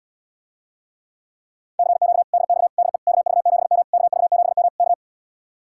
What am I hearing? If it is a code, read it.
Morse code, heard as a keyed tone, about 77 wpm, 700 Hz. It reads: CQ DX DE KV7M KV7M K